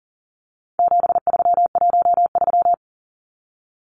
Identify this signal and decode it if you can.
Morse code, heard as a keyed tone, about 40 wpm, 700 Hz. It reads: 7313
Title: Numbers - 4-Digit 40wpm